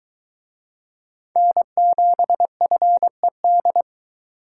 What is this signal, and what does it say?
Morse code, heard as a keyed tone, about 22 wpm, 700 Hz. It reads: N7FED